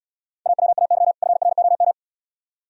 Morse code, heard as a keyed tone, about 76 wpm, 700 Hz. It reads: AWAY FROM